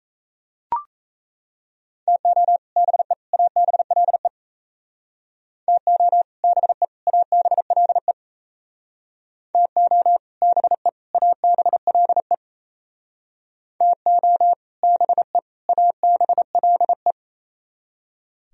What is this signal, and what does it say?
Morse code, heard as a keyed tone, about 37 wpm, 700 Hz. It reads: TO BE ABLE TO BE ABLE TO BE ABLE TO BE ABLE